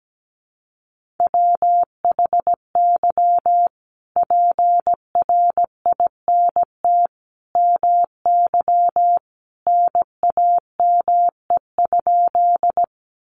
Morse code, heard as a keyed tone, about 17 wpm, 700 Hz. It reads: WHY PRINT MY NAME?
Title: Sentences from Top 1400 Words 17wpm